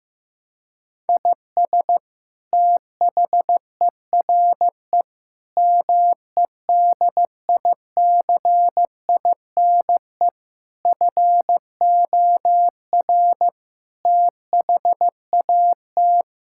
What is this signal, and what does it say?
Morse code, heard as a keyed tone, about 15 wpm, 700 Hz. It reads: IS THERE MEDICINE FOR THAT
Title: Sentences from Top 1000 Words - Review Part 6 - 15wpm